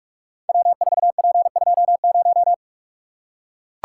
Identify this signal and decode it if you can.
Morse code, heard as a keyed tone, about 44 wpm, 700 Hz. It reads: W4P20
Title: Code Groups - 5 Letter:Numbers - 45wpm